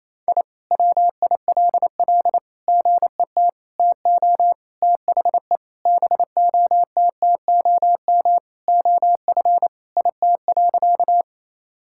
Morse code, heard as a keyed tone, about 27 wpm, 700 Hz. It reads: I WILL GET TO THE BOTTOM OF IT.
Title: Sentences from Top 1000 Words - Review Part 6 - 28wpm